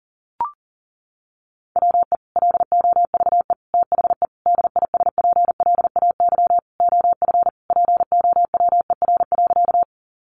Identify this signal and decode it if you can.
Morse code, heard as a keyed tone, about 40 wpm, 700 Hz. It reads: WE LOVE THE DISPLAY OF POWER.
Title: Sentences from Top 1200 Words 40wpm